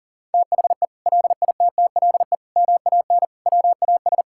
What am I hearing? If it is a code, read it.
Morse code, heard as a keyed tone, about 40 wpm, 700 Hz. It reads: THE LITTLE MAN WAS
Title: Sentences from Top 1600 Words 40wpm